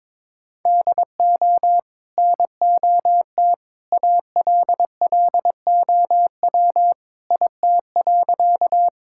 Morse code, heard as a keyed tone, about 22 wpm, 700 Hz. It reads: DO NOT ALLOW IT.
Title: Sentences from Top 1000 Words - Review Part 5 - 22wpm